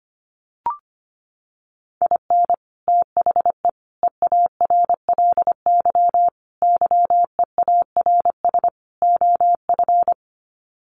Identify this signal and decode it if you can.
Morse code, heard as a keyed tone, about 25 wpm, 690 Hz. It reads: IN THE EARLY YEARS OF
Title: Sets of 5 Words Encore 25wpm